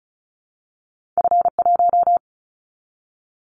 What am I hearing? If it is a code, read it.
Morse code, heard as a keyed tone, about 36 wpm, 700 Hz. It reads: F1
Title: Callsign Prefixes for Top 50 DXCC - 35wpm